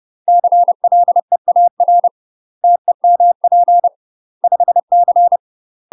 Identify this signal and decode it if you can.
Morse code, heard as a keyed tone, about 30 wpm, 700 Hz. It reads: CLEAR TEMP 5C
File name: UK MASTER QSOs TRAINING / 30WPM / UK MASTER QSOs 0035